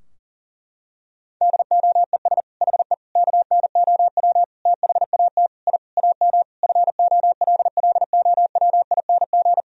Morse code, heard as a keyed tone, about 40 wpm, 700 Hz. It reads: DOES HE KNOW THAT I AM FOLLOWING
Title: Sentences from Top 1000 Words - Review Part 4 - 40wpm